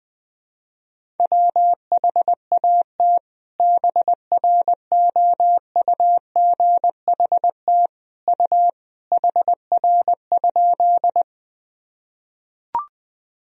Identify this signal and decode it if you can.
Morse code, heard as a keyed tone, about 21 wpm, 700 Hz. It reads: WHAT BROUGHT U HR?
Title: Sentences from Top 500 Words and Abbreviations 20wpm